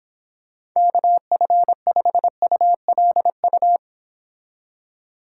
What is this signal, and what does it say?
Morse code, heard as a keyed tone, about 26 wpm, 700 Hz. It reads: KF5ULU